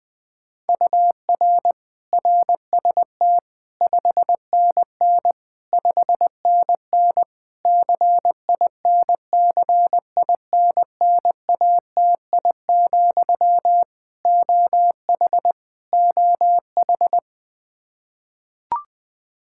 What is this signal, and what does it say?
Morse code, heard as a keyed tone, about 21 wpm, 700 Hz. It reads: UR RST 5NN 5NN CINCINNATI, OH OH
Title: Practice QSO - Exchange 1 - Beginning - 20wpm